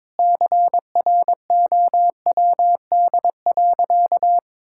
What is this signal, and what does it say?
Morse code, heard as a keyed tone, about 22 wpm, 700 Hz. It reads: CROWD.
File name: practice / 22wpm / Sentences with 4-Word Phrases - 22wpm